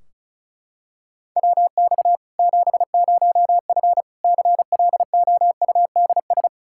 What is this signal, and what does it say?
Morse code, heard as a keyed tone, about 34 wpm, 700 Hz. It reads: WX 70F CLOUDS
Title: K1USN SST QSO Practice 35wpm